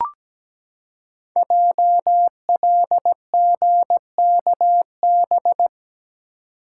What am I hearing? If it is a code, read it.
Morse code, heard as a keyed tone, about 17 wpm, 700 Hz. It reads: JLGKB
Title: Code Groups - 5 Letters - 17wpm